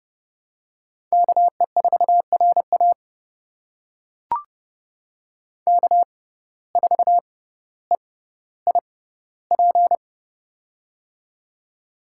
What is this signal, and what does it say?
Morse code, heard as a keyed tone, about 30 wpm, spaced as for 15 wpm, 700 Hz. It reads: KE4RA K4EIP
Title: USA Call Signs - 5 Characters - 6x Character Spacing - 30wpm